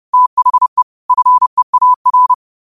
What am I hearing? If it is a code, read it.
Morse code, heard as a keyed tone, about 30 wpm, 1000 Hz. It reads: THE FEAR
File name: qotdfiles / 30wpm